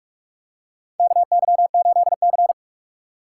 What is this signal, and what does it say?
Morse code, heard as a keyed tone, about 46 wpm, 700 Hz. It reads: KY8C
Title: USA Call Signs - 4 Characters - 6x Character Spacing - 45wpm